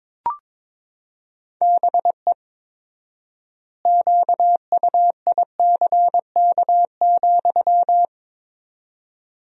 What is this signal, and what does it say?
Morse code, heard as a keyed tone, about 22 wpm, 700 Hz. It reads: BE QUICK,
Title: 4x Word-Spacing - Sentences from Top 500 Words - 22wpm